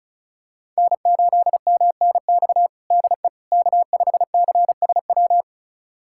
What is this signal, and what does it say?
Morse code, heard as a keyed tone, about 36 wpm, 690 Hz. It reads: N8MNX DE K5CSW
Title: USA Callsign DE Callsign 35wpm